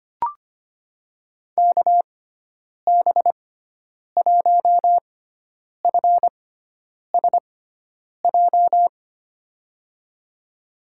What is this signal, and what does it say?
Morse code, heard as a keyed tone, about 25 wpm, spaced as for 9 wpm, 700 Hz. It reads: KB1FSJ
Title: USA Call Signs - 6 Characters - 6x Character Spacing - 25wpm